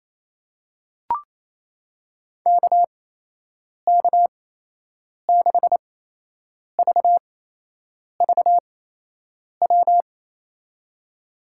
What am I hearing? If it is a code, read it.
Morse code, heard as a keyed tone, about 27 wpm, spaced as for 8 wpm, 700 Hz. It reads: KK6VVW
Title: USA Call Signs - 6 Characters - 8x Character Spacing - 28wpm